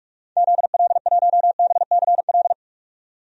Morse code, heard as a keyed tone, about 46 wpm, 700 Hz. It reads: ZL1BCL
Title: DXCC-100 Call Signs - 6 Characters - 45wpm